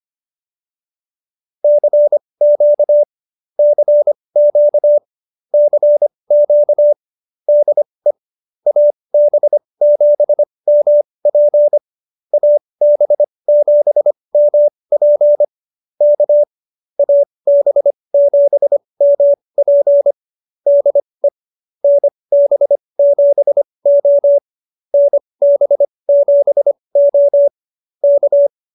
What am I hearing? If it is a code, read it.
Morse code, heard as a keyed tone, about 25 wpm, spaced as for 20 wpm, 590 Hz. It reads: CQ CQ CQ DE AB7MP AB7MP K AB7MP DE NB7O NB7O K